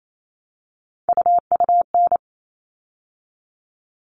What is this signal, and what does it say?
Morse code, heard as a keyed tone, about 28 wpm, 700 Hz. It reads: UUN
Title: All Cut Numbers - 3-Digit 28wpm